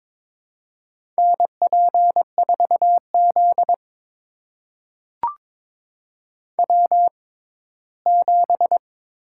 Morse code, heard as a keyed tone, about 22 wpm, spaced as for 15 wpm, 700 Hz. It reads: NP4Z W7
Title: USA Call Signs - 4 Characters - 6x Character Spacing - 22wpm